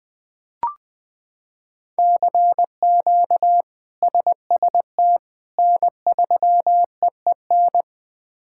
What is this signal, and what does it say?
Morse code, heard as a keyed tone, about 21 wpm, 700 Hz. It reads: CQ SST N3EEN
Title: K1USN SST QSO Practice - 20wpm